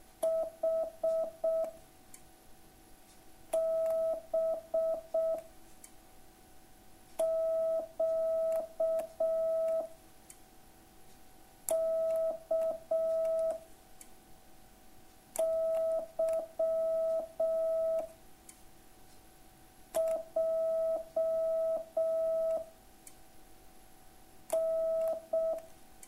Morse code, heard as a keyed tone, about 6 wpm, 660 Hz. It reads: H B Q K Y J N